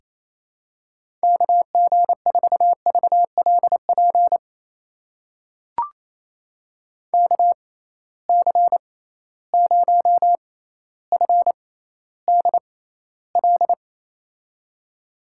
Morse code, heard as a keyed tone, about 28 wpm, spaced as for 14 wpm, 700 Hz. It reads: KG4VLP KC0FDL
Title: USA Call Signs - 6 Characters - 6x Character Spacing - 28wpm